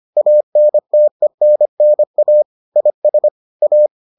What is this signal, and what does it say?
Morse code, heard as a keyed tone, about 25 wpm, 600 Hz. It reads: ANTENNA IS A